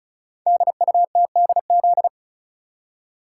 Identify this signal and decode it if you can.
Morse code, heard as a keyed tone, about 36 wpm, 700 Hz. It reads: DUTDZ
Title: Code Groups - 5 Letters - 35wpm